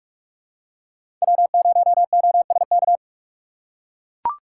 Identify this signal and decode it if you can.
Morse code, heard as a keyed tone, about 43 wpm, 700 Hz. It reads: W0OSK
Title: USA Call Signs - 5 Characters - 2x Character Spacing - 45wpm